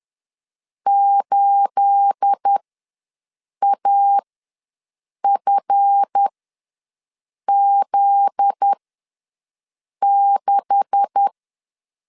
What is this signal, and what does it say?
Morse code, heard as a keyed tone, about 10 wpm, spaced as for 6 wpm, 800 Hz. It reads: 8AFZ6
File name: LEZIONI / 9 286 ws 10wpm 4032